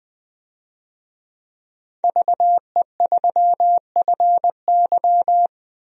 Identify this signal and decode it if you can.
Morse code, heard as a keyed tone, about 20 wpm, 700 Hz. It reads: VE3FY
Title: DXCC-100 Call Signs - 5 Characters - 20wpm